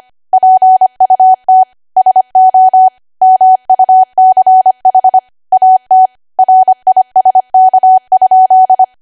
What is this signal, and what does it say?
Morse code, heard as a keyed tone, about 25 wpm, 750 Hz. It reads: PUT SO MUCH AT RISK?
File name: W1AW-Level3/140218 25WPM